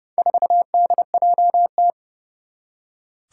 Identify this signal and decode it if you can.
Morse code, heard as a keyed tone, about 31 wpm, 700 Hz. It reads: 4DJT
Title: Four Challenging Letter-Number Combinations 30wpm